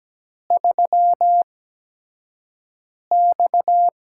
Morse code, heard as a keyed tone, about 17 wpm, spaced as for 5 wpm, 700 Hz. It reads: 3X
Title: USA Call Signs - 4 Characters - 8x Character Spacing - 17wpm